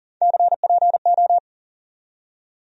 Morse code, heard as a keyed tone, about 39 wpm, 700 Hz. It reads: CPO
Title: USA Call Sign Postfixes - 3 Letters 40wpm